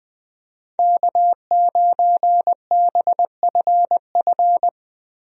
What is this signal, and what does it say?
Morse code, heard as a keyed tone, about 21 wpm, 700 Hz. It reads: K9BFF
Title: USA Call Signs - 5 Characters - Repeated 2x - 20wpm